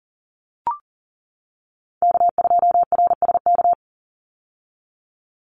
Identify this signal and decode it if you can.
Morse code, heard as a keyed tone, about 41 wpm, 700 Hz. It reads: K2RSK
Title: Call Signs - 40wpm